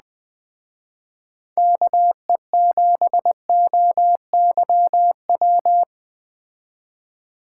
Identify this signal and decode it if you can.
Morse code, heard as a keyed tone, about 20 wpm, 700 Hz. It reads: KE7OYW
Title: Speed Racing - USA 2x3 Call Signs 17wpm